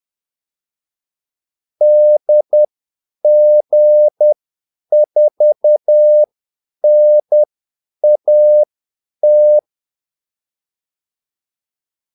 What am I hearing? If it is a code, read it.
Morse code, heard as a keyed tone, about 10 wpm, spaced as for 8 wpm, 600 Hz. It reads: DG4NAT
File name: Callsigns-10-8wpm